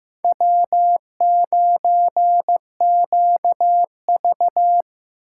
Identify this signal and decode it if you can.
Morse code, heard as a keyed tone, about 18 wpm, 700 Hz. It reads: W9QV
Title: Speed Racing - USA 1x2 Call Signs 15wpm